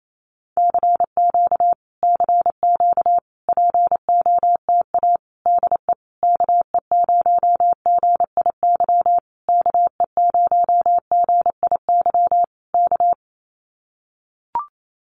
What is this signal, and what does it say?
Morse code, heard as a keyed tone, about 28 wpm, 700 Hz. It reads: CQ CQ POTA DE KE0GIY KE0GIY K
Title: Common QSO Phrases Encore 28wpm